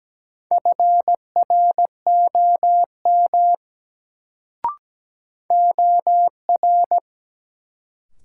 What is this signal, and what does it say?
Morse code, heard as a keyed tone, about 17 wpm, 700 Hz. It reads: FROM OR
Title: Sentences from Top 100 Words - New Word Drill - 17wpm